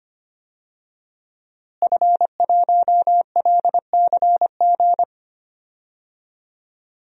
Morse code, heard as a keyed tone, about 25 wpm, 700 Hz. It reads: F1LCG
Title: DXCC-100 Call Signs - 5 Characters - Repeated 3x - 25wpm